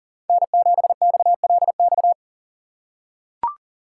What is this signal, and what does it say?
Morse code, heard as a keyed tone, about 40 wpm, 700 Hz. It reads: N7XLX
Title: USA Call Signs - 5 Characters - 6x Character Spacing - 40wpm